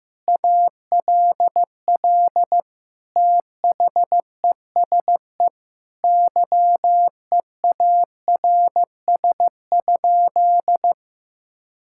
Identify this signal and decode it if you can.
Morse code, heard as a keyed tone, about 15 wpm, 700 Hz. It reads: ALL THESE YEARS?
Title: Sentences from Top 1000 Words - Review Part 6 - 15wpm